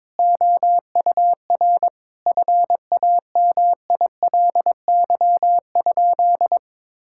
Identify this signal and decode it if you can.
Morse code, heard as a keyed tone, about 22 wpm, 700 Hz. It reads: OUR FAMILY?
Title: Sentences from Top 1000 Words - Review Part 6 - 22wpm